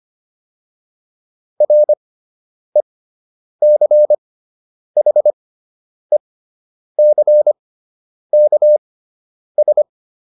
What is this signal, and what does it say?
Morse code, heard as a keyed tone, about 25 wpm, spaced as for 9 wpm, 600 Hz. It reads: RECHECKS